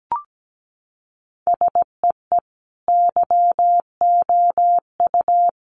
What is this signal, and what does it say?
Morse code, heard as a keyed tone, about 17 wpm, 700 Hz. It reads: SEE YOU
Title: Sentences from Top 700 Words 17wpm